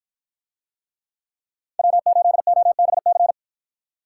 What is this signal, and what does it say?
Morse code, heard as a keyed tone, about 51 wpm, 700 Hz. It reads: W8OBC